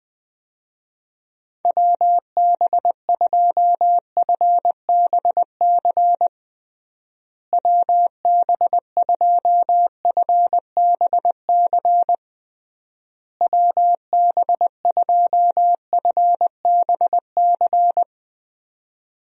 Morse code, heard as a keyed tone, about 20 wpm, 700 Hz. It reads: WB2FBC WB2FBC WB2FBC
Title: USA Call Signs - 6 Characters - Repeated 3x - 20wpm